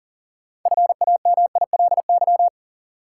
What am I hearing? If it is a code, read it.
Morse code, heard as a keyed tone, about 40 wpm, 700 Hz. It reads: FAMILY
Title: Sentences from Top 400 Words - New Word Drill - 40wpm